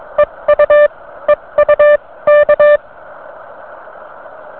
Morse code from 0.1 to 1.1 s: EU